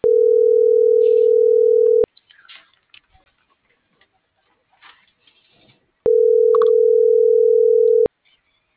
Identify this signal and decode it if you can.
Ringback tone (North America), 440 + 480 Hz.